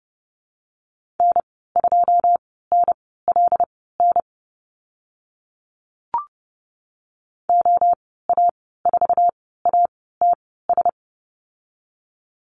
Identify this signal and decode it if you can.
Morse code, heard as a keyed tone, about 30 wpm, spaced as for 17 wpm, 700 Hz. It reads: N2NLN OA4ATS
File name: Lesson 15g - Number 5 - Call Signs 30wpm